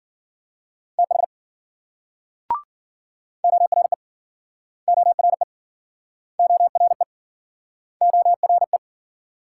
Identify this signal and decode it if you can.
Morse code, heard as a keyed tone, about 53 wpm, 700 Hz. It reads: TH ORE ORE ORE ORE